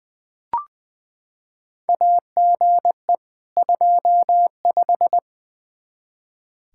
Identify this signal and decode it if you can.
Morse code, heard as a keyed tone, about 20 wpm, 700 Hz. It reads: AGE 25
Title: Practice QSO - Exchange 3 - More Info - 20wpm